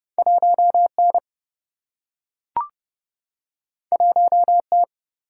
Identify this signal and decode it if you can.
Morse code, heard as a keyed tone, about 30 wpm, 700 Hz. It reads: 1N 1T
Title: Single-Letter Single-Number 30wpm